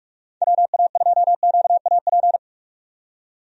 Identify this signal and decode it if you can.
Morse code, heard as a keyed tone, about 45 wpm, 700 Hz. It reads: WA2QAP
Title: USA Call Signs - 6 Characters - 45wpm